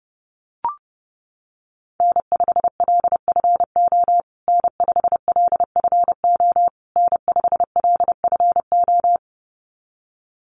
Morse code, heard as a keyed tone, about 28 wpm, 700 Hz. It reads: N5LFO N5LFO N5LFO
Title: USA Call Signs Repeated 3x 30wpm